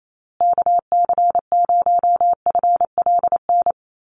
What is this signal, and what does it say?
Morse code, heard as a keyed tone, about 28 wpm, 700 Hz. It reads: KC0FLN
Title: USA Call Signs Repeated 3x 28wpm